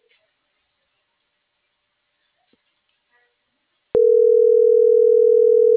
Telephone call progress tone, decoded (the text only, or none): ringback tone